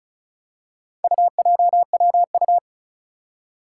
Morse code, heard as a keyed tone, about 36 wpm, 700 Hz. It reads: UJWU
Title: Four Challenging Letter-Number Combinations 35wpm